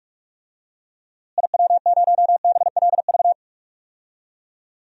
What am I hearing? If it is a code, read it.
Morse code, heard as a keyed tone, about 47 wpm, 700 Hz. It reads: IW0BLV
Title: DXCC-100 Call Signs - 6 Characters - Repeated 2x - 45wpm